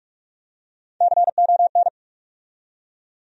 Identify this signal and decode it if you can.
Morse code, heard as a keyed tone, about 43 wpm, 700 Hz. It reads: CON
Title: USA Call Sign Postfixes - 3 Letters 45wpm